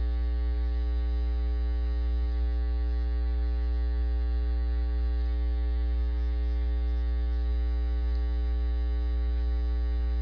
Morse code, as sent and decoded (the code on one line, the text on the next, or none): none